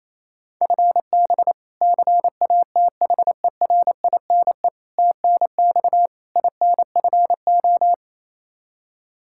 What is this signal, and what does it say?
Morse code, heard as a keyed tone, about 28 wpm, 700 Hz. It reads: FB CATHERINE TNX INFO
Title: K1USN SST QSO Practice 28wpm